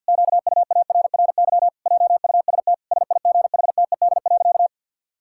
Morse code, heard as a keyed tone, about 50 wpm, 700 Hz. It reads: QUARRY JUST SIGHTED.
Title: Pangram Challenge v2 - 50wpm